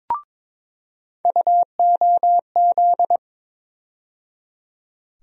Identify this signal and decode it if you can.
Morse code, heard as a keyed tone, about 22 wpm, 700 Hz. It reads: UOZ
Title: USA Call Sign Postfixes - 3 Letters 22wpm